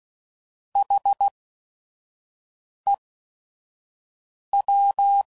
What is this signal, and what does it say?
Morse code, heard as a keyed tone, about 16 wpm, spaced as for 5 wpm, 800 Hz. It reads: HEW